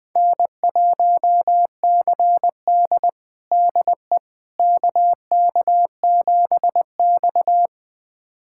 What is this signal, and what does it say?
Morse code, heard as a keyed tone, about 20 wpm, 700 Hz. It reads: N1CD DE KK7X
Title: USA Callsign DE Callsign - 4 Characters - 20wpm